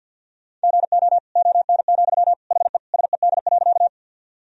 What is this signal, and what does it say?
Morse code, heard as a keyed tone, about 47 wpm, 700 Hz. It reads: GO ON, HE SED.
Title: Sentences from Top 100 Words and Abbreviations 50wpm